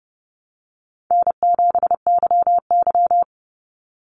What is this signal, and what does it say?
Morse code, heard as a keyed tone, about 30 wpm, 700 Hz. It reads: N7YY
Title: Easy Call Signs - 30wpm